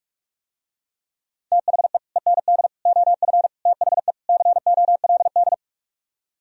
Morse code, heard as a keyed tone, about 44 wpm, 700 Hz. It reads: THE END OF THE COLD